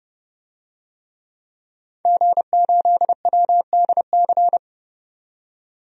Morse code, heard as a keyed tone, about 30 wpm, 700 Hz. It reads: G8WDC
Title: DXCC-100 Call Signs - 5 Characters - 30wpm